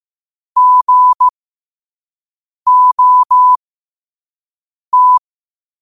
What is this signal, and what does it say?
Morse code, heard as a keyed tone, about 15 wpm, spaced as for 5 wpm, 1000 Hz. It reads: GOT
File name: qotdfiles / 5wpm